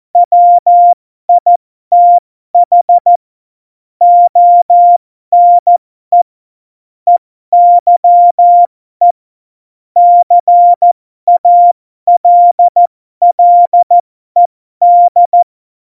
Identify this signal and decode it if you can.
Morse code, heard as a keyed tone, about 14 wpm, spaced as for 12 wpm, 700 Hz. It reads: WITH ONE EYE CALLED
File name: Morsekurs / Witze